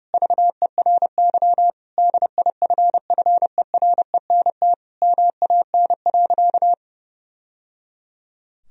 Morse code, heard as a keyed tone, about 30 wpm, 700 Hz. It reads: VERY DIFFERENT MAN.
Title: Sentences from Top 1000 Words - Review Part 3 - 30wpm